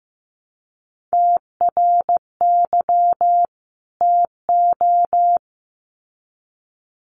Morse code, heard as a keyed tone, about 15 wpm, 700 Hz. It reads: TRY TO
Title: Sets of 2 Words 15wpm